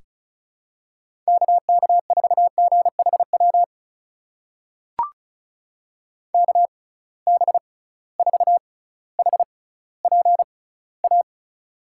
Morse code, heard as a keyed tone, about 36 wpm, spaced as for 18 wpm, 700 Hz. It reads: KK4GHW KB4HPA
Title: USA Call Signs - 6x Character Spacing - 35wpm